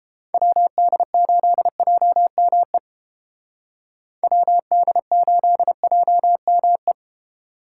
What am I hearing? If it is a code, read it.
Morse code, heard as a keyed tone, about 30 wpm, 700 Hz. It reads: WD8JME WD8JME